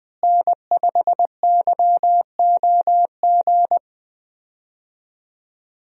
Morse code, heard as a keyed tone, about 20 wpm, 700 Hz. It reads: N5YOG
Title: USA Call Signs - 5 Characters - Repeated 2x - 20wpm